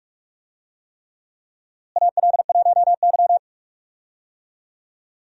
Morse code, heard as a keyed tone, about 46 wpm, 700 Hz. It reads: AL1Y
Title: USA Call Signs - 4 Characters - Repeated 3x - 45wpm